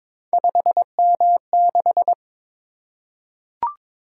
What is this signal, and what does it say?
Morse code, heard as a keyed tone, about 22 wpm, 700 Hz. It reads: 5M6
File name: practice / 22wpm / Three Letter-Number Combinations - 22wpm